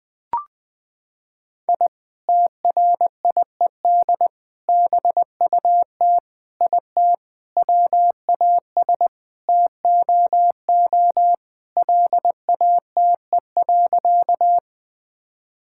Morse code, heard as a keyed tone, about 20 wpm, 700 Hz. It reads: I TRIED BUT IT WAS TOO LATE.
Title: Sentences with 3-Word Phrases - 20wpm